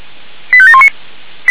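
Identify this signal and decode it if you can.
Five-tone selective call (ZVEI), heard as quick runs of five tones, tones of about 70 ms. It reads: 85719
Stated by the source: ZVEI five-tone selective call received by SDR.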